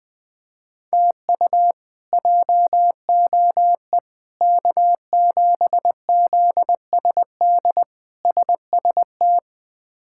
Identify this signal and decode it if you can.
Morse code, heard as a keyed tone, about 20 wpm, 700 Hz. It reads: TU JOE K7ZSD SST
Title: K1USN SST QSO Practice - 20wpm